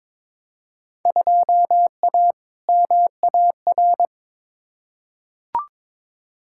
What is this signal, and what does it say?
Morse code, heard as a keyed tone, about 21 wpm, 700 Hz. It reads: 2A MAR
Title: Field Day Exchanges - 22wpm